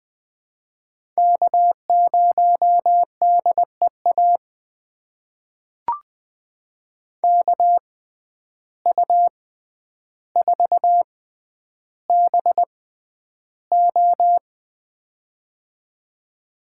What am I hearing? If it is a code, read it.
Morse code, heard as a keyed tone, about 20 wpm, spaced as for 10 wpm, 700 Hz. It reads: K0DEA KU4BO
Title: USA Call Signs - 5 Characters - 6x Character Spacing - 20wpm